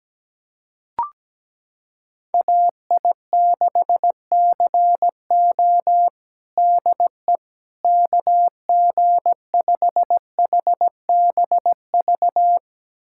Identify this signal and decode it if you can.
Morse code, heard as a keyed tone, about 17 wpm, 700 Hz. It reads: AI6CO DE KG5HBV
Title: USA Callsign DE Callsign 17wpm